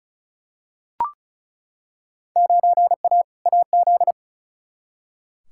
Morse code, heard as a keyed tone, about 36 wpm, 700 Hz. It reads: 9A AZ
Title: Field Day Exchanges - 35wpm